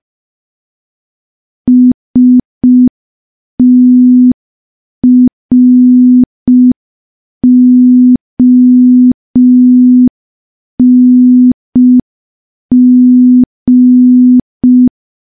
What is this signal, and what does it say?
Morse code, heard as a keyed tone, about 5 wpm, 250 Hz. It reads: STRONG
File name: morse / morse-training-05WPM